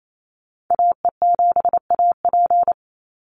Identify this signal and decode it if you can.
Morse code, heard as a keyed tone, about 28 wpm, 700 Hz. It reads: AE7AP
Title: NA SOTA Activator Callsigns 28wpm